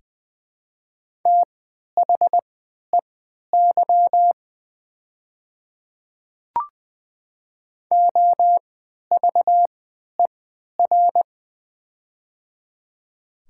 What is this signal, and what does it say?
Morse code, heard as a keyed tone, about 20 wpm, spaced as for 11 wpm, 700 Hz. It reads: THEY OVER
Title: 3x Character Spacing - Top 100 Words - 20wpm